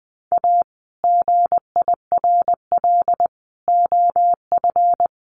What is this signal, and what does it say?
Morse code, heard as a keyed tone, about 19 wpm, 700 Hz. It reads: A GIRL OF
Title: Sentences from Top 1200 Words 20wpm